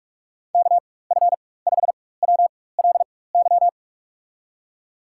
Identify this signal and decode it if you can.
Morse code, heard as a keyed tone, about 47 wpm, spaced as for 21 wpm, 700 Hz. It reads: KF5WLY
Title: USA Call Signs - 6 Characters - 4x Character Spacing - 45wpm